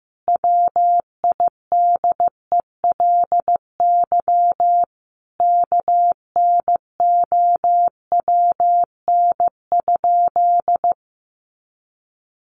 Morse code, heard as a keyed tone, about 15 wpm, 700 Hz. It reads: WIDELY KNOWN?
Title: Sentences from Top 1200 Words 15wpm